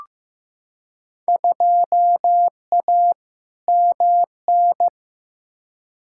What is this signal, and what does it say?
Morse code, heard as a keyed tone, about 15 wpm, 700 Hz. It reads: 2A MN
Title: Field Day Exchanges - 15wpm